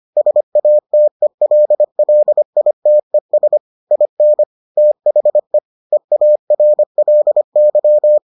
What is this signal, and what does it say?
Morse code, heard as a keyed tone, about 25 wpm, 600 Hz. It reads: SATELLITES IN THE EARLY